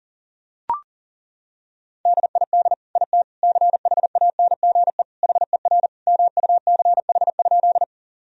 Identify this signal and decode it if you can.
Morse code, heard as a keyed tone, about 40 wpm, 700 Hz. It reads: DID IT CHANGE HER MUCH?
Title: Sentences from Top 1000 Words - Review Part 1 - 40wpm